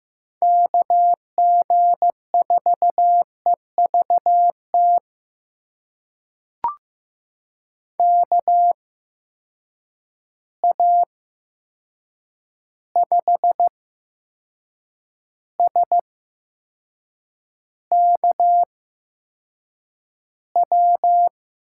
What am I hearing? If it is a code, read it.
Morse code, heard as a keyed tone, about 15 wpm, spaced as for 6 wpm, 700 Hz. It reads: KG4EVT KA5SKW
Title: USA Call Signs - 6 Characters - 8x Character Spacing - 15wpm